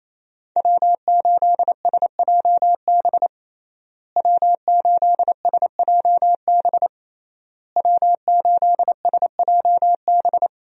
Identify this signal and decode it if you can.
Morse code, heard as a keyed tone, about 27 wpm, 700 Hz. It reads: W8SJB W8SJB W8SJB
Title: USA Call Signs - 5 Characters - Repeated 3x - 28wpm